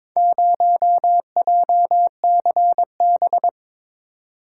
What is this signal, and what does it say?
Morse code, heard as a keyed tone, about 22 wpm, 700 Hz. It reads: 0JCB
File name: practice / 22wpm / Four Letter-Number Combinations - 22wpm